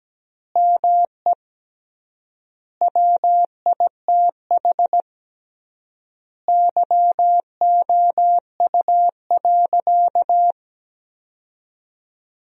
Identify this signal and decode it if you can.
Morse code, heard as a keyed tone, about 17 wpm, 700 Hz. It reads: ME WITH YOU.
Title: 3x Word-Spacing - Sentences from Top 500 Words - 17wpm